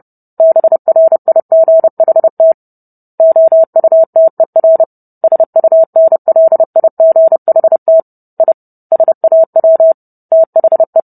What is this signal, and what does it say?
Morse code, heard as a keyed tone, about 30 wpm, 650 Hz. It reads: BRIGHT OUTER SUNLIGHT I SAW THE